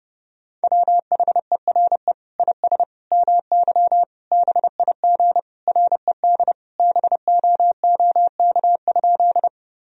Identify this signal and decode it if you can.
Morse code, heard as a keyed tone, about 30 wpm, 700 Hz. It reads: WHERE IS MY BIG RED BOOK?
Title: Sentences from Top 1000 Words - Review Part 2 - 30wpm